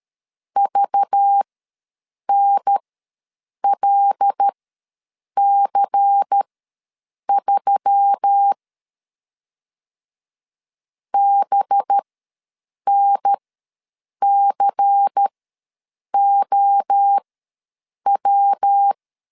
Morse code, heard as a keyed tone, about 13 wpm, spaced as for 7 wpm, 800 Hz. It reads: VNLC3 BNCOW
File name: LEZIONI / 5 LBVC / 5 LBVC ws 12wpm 3159